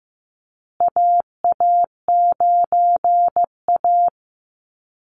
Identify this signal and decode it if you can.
Morse code, heard as a keyed tone, about 15 wpm, 700 Hz. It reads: AA9A
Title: Call Signs Repeated 2x - 15wpm